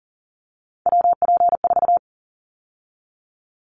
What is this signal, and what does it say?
Morse code, heard as a keyed tone, about 40 wpm, 700 Hz. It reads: WP4
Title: Callsign Prefixes for Top 25 DXCC - 40wpm